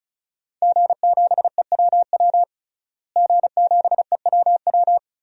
Morse code, heard as a keyed tone, about 36 wpm, 700 Hz. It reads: G7EWW G7EWW
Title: DXCC-100 Call Signs - 5 Characters - Repeated 2x - 35wpm